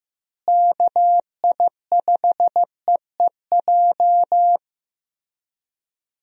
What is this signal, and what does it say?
Morse code, heard as a keyed tone, about 16 wpm, 700 Hz. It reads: KI5EEJ
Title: USA Call Signs - 6 Characters - Repeated 3x - 15wpm